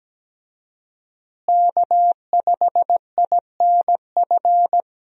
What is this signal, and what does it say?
Morse code, heard as a keyed tone, about 17 wpm, 700 Hz. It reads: K5INF
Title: USA Call Signs - 5 Characters - Repeated 3x - 17wpm